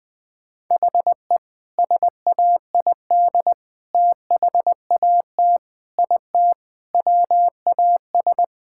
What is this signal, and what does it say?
Morse code, heard as a keyed tone, about 20 wpm, 700 Hz. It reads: HE SAID THAT IT WAS
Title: Sentences with 2-Word Phrases - Spoken Beginning - 20wpm